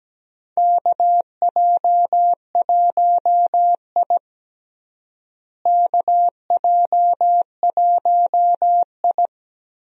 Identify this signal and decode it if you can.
Morse code, heard as a keyed tone, about 17 wpm, 700 Hz. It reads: KJ1I KJ1I
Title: USA Call Signs - 4 Characters - Repeated 3x - 17wpm